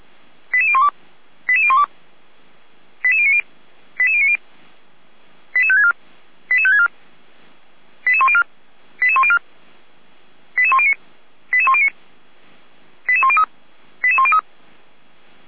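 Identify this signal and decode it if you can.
Five-tone selective call (ZVEI), heard as quick runs of five tones, tones of about 70 ms. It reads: 88012 88012 88090 88090 88564 88564 88104 88104 88109 88109 88103 88103
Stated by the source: Received on an SDR.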